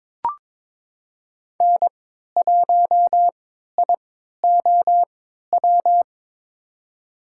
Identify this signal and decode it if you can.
Morse code, heard as a keyed tone, about 21 wpm, spaced as for 12 wpm, 700 Hz. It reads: N1IOW
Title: USA Call Signs - 5 Characters - 3x Character Spacing - 22wpm